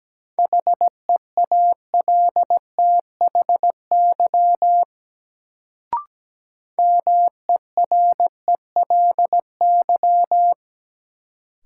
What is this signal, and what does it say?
Morse code, heard as a keyed tone, about 17 wpm, 700 Hz. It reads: HEALTHY MERELY
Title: Sentences from Top 1400 Words - New Word Drill Alt - 17wpm